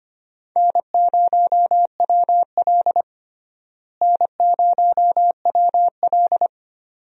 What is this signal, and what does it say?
Morse code, heard as a keyed tone, about 25 wpm, 700 Hz. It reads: N0WL N0WL
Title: USA Call Signs - 4 Characters - Repeated 2x - 25wpm